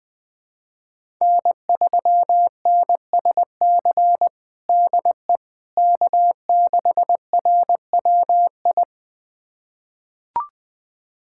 Morse code, heard as a keyed tone, about 20 wpm, 700 Hz. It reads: N3NSC DE K6RWI
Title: USA Callsign DE Callsign - 5 Characters - 20wpm